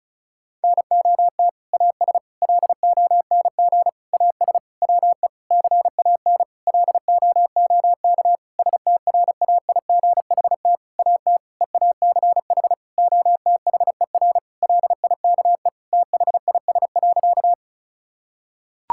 Morse code, heard as a keyed tone, about 34 wpm, 700 Hz. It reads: NOT AS LONG AS WE CAN LOOK STRAIGHT AT EACH OTHER LIKE THIS.
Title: Sentences from Top 1000 Words - Review Part 6 - 35wpm